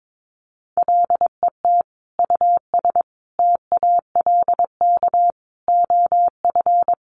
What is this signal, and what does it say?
Morse code, heard as a keyed tone, about 22 wpm, 700 Hz. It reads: LET US TALK OF
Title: Sentences from Top 1000 Words 22wpm